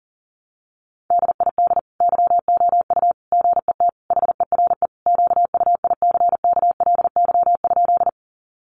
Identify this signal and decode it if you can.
Morse code, heard as a keyed tone, about 40 wpm, 700 Hz. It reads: DID YOU GET HERE QUICKLY?
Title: Sentences from Top 700 Words 40wpm